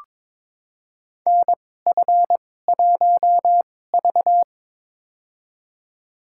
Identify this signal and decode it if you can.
Morse code, heard as a keyed tone, about 22 wpm, spaced as for 16 wpm, 700 Hz. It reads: NF1V
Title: USA Call Signs - 4 Characters - 2x Character Spacing - 22wpm